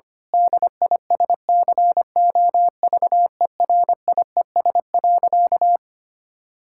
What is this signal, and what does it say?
Morse code, heard as a keyed tone, about 25 wpm, 700 Hz. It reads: DISCOVERIES.
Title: Pangram Challenge v2 - 25wpm